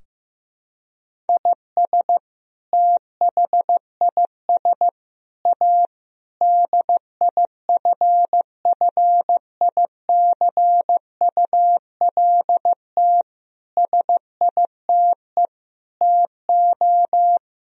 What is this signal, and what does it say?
Morse code, heard as a keyed tone, about 15 wpm, 700 Hz. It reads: IS THIS A DIFFICULT SITE TO
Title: Sentences from Top 1000 Words - Review Part 4 - 15wpm